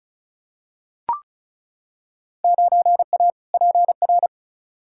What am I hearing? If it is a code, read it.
Morse code, heard as a keyed tone, about 36 wpm, 700 Hz. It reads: 9A PR
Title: Field Day Exchanges Encore - 35wpm